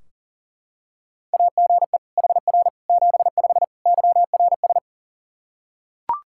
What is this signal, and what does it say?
Morse code, heard as a keyed tone, about 40 wpm, 700 Hz. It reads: AGE HR 75 YRS
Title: K1USN SST QSO Practice 40wpm